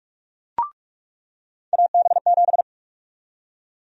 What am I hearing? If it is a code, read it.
Morse code, heard as a keyed tone, about 41 wpm, 700 Hz. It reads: AB7
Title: USA Call Sign Prefixes - 45wpm